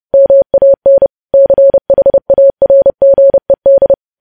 Morse code, heard as a keyed tone, about 30 wpm, 550 Hz. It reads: MAN CHARGED
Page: News Headlines in Morse Code at 30 WPM